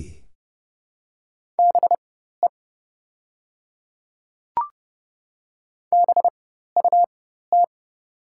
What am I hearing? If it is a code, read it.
Morse code, heard as a keyed tone, about 30 wpm, spaced as for 14 wpm, 700 Hz. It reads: BE BUT
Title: 4x Character Spacing - Top 200 Words - 30wpm